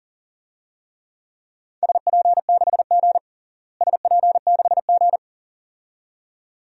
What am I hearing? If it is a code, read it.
Morse code, heard as a keyed tone, about 37 wpm, 700 Hz. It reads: SP6G SP6G
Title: DXCC-100 Call Signs - 4 Characters - Repeated 2x - 40wpm